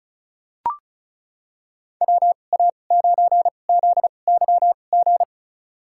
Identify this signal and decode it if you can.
Morse code, heard as a keyed tone, about 36 wpm, spaced as for 25 wpm, 700 Hz. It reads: WA9ZYG
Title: USA Call Signs - 6 Characters - 2x Character Spacing - 35wpm